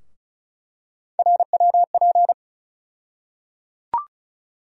Morse code, heard as a keyed tone, about 36 wpm, 700 Hz. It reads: RWP
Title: USA Call Sign Postfixes - 3 Letters 35wpm